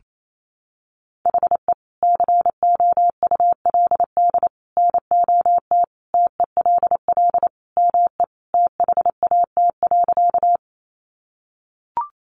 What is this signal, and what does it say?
Morse code, heard as a keyed tone, about 28 wpm, 700 Hz. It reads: HE COULD NOT TELL ME THAT.
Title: Sentences from Top 600 Words 28wpm